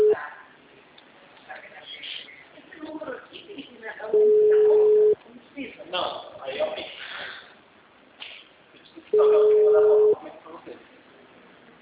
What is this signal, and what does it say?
Ringback tone (Europe, 425 Hz), with speech.
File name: out- -2003- - - .590